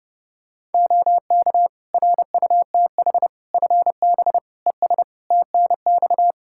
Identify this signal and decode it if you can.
Morse code, heard as a keyed tone, about 30 wpm, 700 Hz. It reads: OK RUTH FB ES TNX
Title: Practice QSO - Exchange 2 - Info - 30wpm